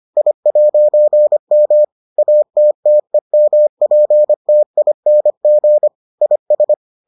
Morse code, heard as a keyed tone, about 25 wpm, 600 Hz. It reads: I'M ATTEMPTING IS